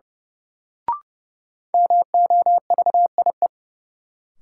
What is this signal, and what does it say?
Morse code, heard as a keyed tone, about 31 wpm, 700 Hz. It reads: MOVIE
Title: Sentences from Top 1000 Words - Review Part 5 - New Word Drill - 30wpm